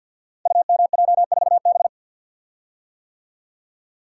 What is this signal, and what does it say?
Morse code, heard as a keyed tone, about 50 wpm, 700 Hz. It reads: UMJ4B
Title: Code Groups - 5 Letter:Numbers - 50wpm